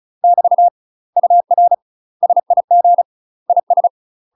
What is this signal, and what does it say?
Morse code, heard as a keyed tone, about 34 wpm, 700 Hz. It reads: <BT> UR SIG IS
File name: UK MASTER QSOs TRAINING / 35WPM / UK MASTER QSOs 0008